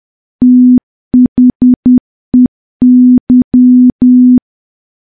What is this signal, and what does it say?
Morse code, heard as a keyed tone, about 10 wpm, 250 Hz. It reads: THEY
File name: morse / morse-training-10WPM